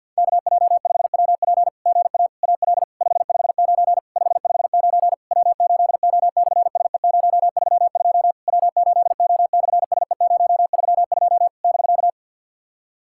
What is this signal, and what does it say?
Morse code, heard as a keyed tone, about 50 wpm, 700 Hz. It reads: KJ5WP GA AL 559 559 W8O/SE032 W8O/SE032 <BK>
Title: SOTA QSO Practice 50wpm